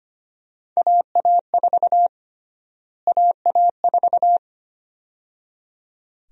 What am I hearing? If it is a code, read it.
Morse code, heard as a keyed tone, about 25 wpm, 700 Hz. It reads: AA4 AA4
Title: USA Call Sign Prefixes - Repeated 2x - 25wpm